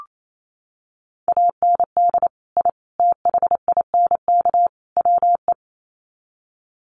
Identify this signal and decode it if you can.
Morse code, heard as a keyed tone, about 28 wpm, 700 Hz. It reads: AND I THINK WE
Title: Sets of 4 Words 28wpm